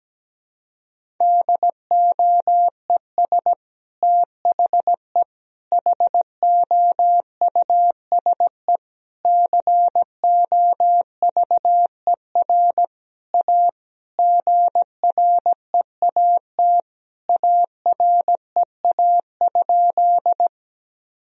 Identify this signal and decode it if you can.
Morse code, heard as a keyed tone, about 17 wpm, 700 Hz. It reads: DOES THE HOUSE COVER A GREAT AREA?
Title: Sentences from Top 1000 Words - Review Part 2 - 17wpm